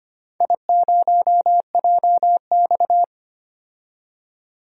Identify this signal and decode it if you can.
Morse code, heard as a keyed tone, about 25 wpm, 700 Hz. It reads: I0JX
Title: DXCC-100 Call Signs - 4 Characters - 25wpm